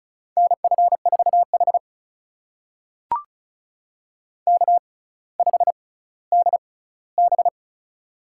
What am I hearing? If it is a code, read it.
Morse code, heard as a keyed tone, about 35 wpm, spaced as for 18 wpm, 700 Hz. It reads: NF4H K5DB
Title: USA Call Signs - 4 Characters - 6x Character Spacing - 35wpm